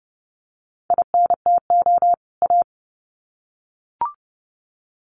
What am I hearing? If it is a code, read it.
Morse code, heard as a keyed tone, about 30 wpm, 700 Hz. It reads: INTO A
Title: Sets of 2 Words 30wpm